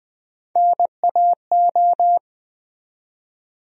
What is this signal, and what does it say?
Morse code, heard as a keyed tone, about 20 wpm, 700 Hz. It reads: NAO
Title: USA Call Sign Postfixes - 3 Letters 20wpm